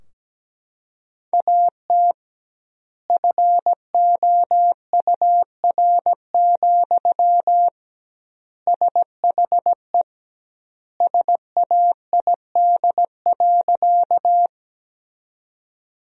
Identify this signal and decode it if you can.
Morse code, heard as a keyed tone, about 17 wpm, 700 Hz. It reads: AT FOUR, SHE SAID.
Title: 2x Word-Spacing - Sentences from Top 300 Words - 17wpm